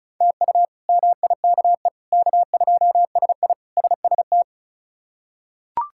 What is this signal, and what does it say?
Morse code, heard as a keyed tone, about 34 wpm, 700 Hz. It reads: TU MIKE K2SI SST
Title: K1USN SST QSO Practice - 35wpm